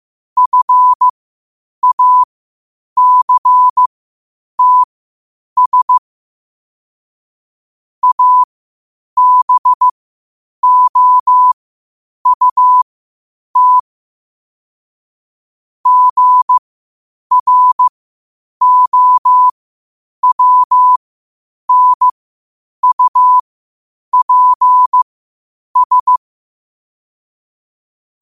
Morse code, heard as a keyed tone, about 15 wpm, spaced as for 8 wpm, 1000 Hz. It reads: FACTS ABOUT GROWNUPS
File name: qotdfiles / 8wpm